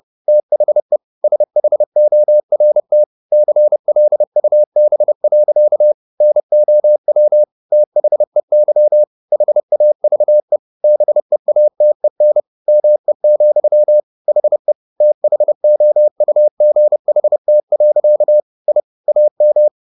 Morse code, heard as a keyed tone, about 30 wpm, 600 Hz. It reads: THE SHORT CLUB. NOW THEY HAVE BEATEN ME, HE THOUGHT. I AM